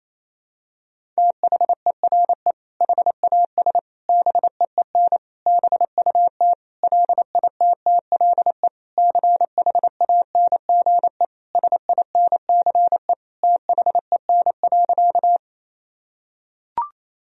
Morse code, heard as a keyed tone, about 28 wpm, 700 Hz. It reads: THERE HAS BEEN BUT LITTLE CHANGE SINCE THEN.
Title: Sentences from Top 1000 Words - Review Part 1 - 28wpm